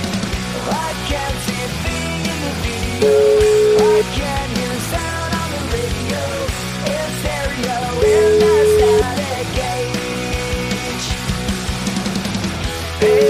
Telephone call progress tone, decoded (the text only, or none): ringback tone